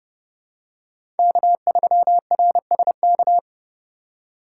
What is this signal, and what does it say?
Morse code, heard as a keyed tone, about 31 wpm, 700 Hz. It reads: K3RSK
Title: USA Call Signs - 5 Characters - 2x Character Spacing - 30wpm